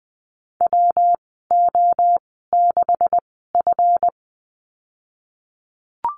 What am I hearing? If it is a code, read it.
Morse code, heard as a keyed tone, about 20 wpm, spaced as for 15 wpm, 700 Hz. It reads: WO6F
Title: 140 Lesson 23g - Letter f - Call Signs 20wpm